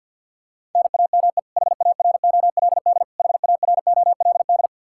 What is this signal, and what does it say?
Morse code, heard as a keyed tone, about 51 wpm, 700 Hz. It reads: NAME HAROLD HAROLD
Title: Practice QSO - Exchange 1 - Beginning - 50wpm